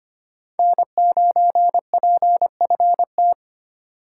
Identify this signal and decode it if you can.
Morse code, heard as a keyed tone, about 25 wpm, 700 Hz. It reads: N9PFT
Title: USA Call Signs - 5 Characters - 2x Character Spacing - 25wpm